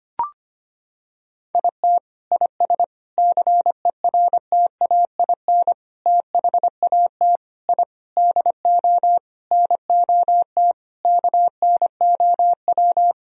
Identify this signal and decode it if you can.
Morse code, heard as a keyed tone, about 25 wpm, 700 Hz. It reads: IT IS CERTAIN THAT I DO NOT KNOW
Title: Sentences from Top 1000 Words - Review Part 2 - 25wpm